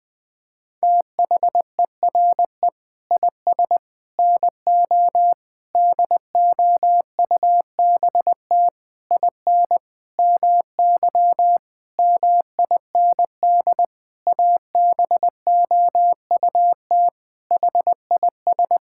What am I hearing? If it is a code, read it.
Morse code, heard as a keyed tone, about 20 wpm, 700 Hz. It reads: THERE IS NO DOUBT IN MY MIND ABOUT HIS
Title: Sentences with 7-Word Phrases - 20wpm